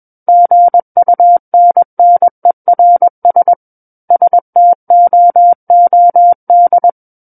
Morse code, heard as a keyed tone, about 21 wpm, 700 Hz. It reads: GUNNERS STOOD
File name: Book One / 12-Chapter 12 - 21 WPM